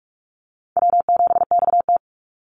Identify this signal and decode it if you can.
Morse code, heard as a keyed tone, about 44 wpm, 700 Hz. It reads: W7XT
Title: Easy Call Signs Repeated 3x - 45wpm